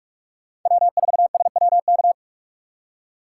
Morse code, heard as a keyed tone, about 45 wpm, 700 Hz. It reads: W4SWK
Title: USA Call Signs - 2x Character Spacing - 45wpm